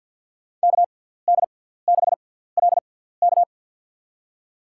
Morse code, heard as a keyed tone, about 51 wpm, spaced as for 17 wpm, 700 Hz. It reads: KD6LK
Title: USA Call Signs - 5 Characters - 6x Character Spacing - 50wpm